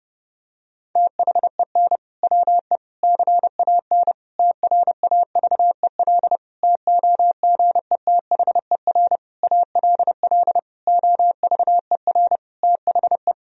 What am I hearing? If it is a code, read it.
Morse code, heard as a keyed tone, about 30 wpm, 700 Hz. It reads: THEN WE CAN TRAVEL TOGETHER ALL OVER THE
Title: Sentences from Top 1000 Words - Review Part 2 - 30wpm